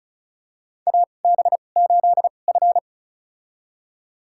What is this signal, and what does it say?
Morse code, heard as a keyed tone, about 34 wpm, spaced as for 25 wpm, 700 Hz. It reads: AB8F
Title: USA Call Signs - 2x Character Spacing - 35wpm